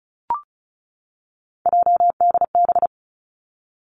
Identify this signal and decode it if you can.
Morse code, heard as a keyed tone, about 36 wpm, 700 Hz. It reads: JDB
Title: Three Challenging Letter-Number Combinations 35wpm